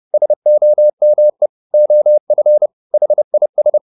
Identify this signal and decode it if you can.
Morse code, heard as a keyed tone, about 30 wpm, 600 Hz. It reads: SOME OF HIS